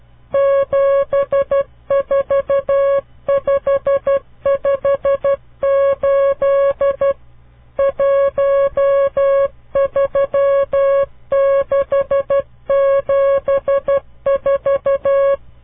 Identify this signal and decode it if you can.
Morse code, heard as a keyed tone, about 12 wpm, 550 Hz. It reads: 74558 13674
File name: CW Learning Programs / farnsworth / unit 5